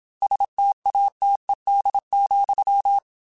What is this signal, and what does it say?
Morse code, heard as a keyed tone, about 27 wpm, 800 Hz. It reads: STATED,
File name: Technical Subjects / Technical Subjects-mp3s / XC @25